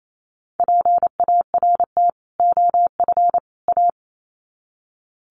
Morse code, heard as a keyed tone, about 28 wpm, 700 Hz. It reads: PART OF A
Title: Sets of 3 Words 28wpm